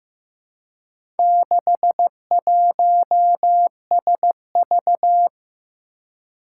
Morse code, heard as a keyed tone, about 15 wpm, 700 Hz. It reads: 61SV
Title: Four Challenging Letter-Number Combinations 15wpm